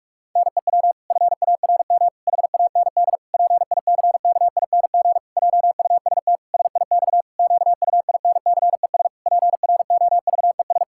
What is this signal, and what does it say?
Morse code, heard as a keyed tone, about 46 wpm, 700 Hz. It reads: NEW FARM HAND PICKING JUST SIX QUINCES PROVES
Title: Pangram Challenge v2 - 45wpm